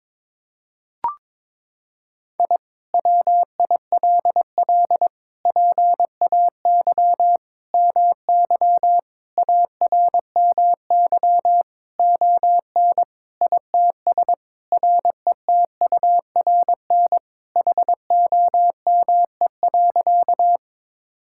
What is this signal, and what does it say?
Morse code, heard as a keyed tone, about 22 wpm, 700 Hz. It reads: I WILL PAY MY ARMY ON ITS RETURN HOME.
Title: Sentences from Top 1000 Words - Review Part 5 - 22wpm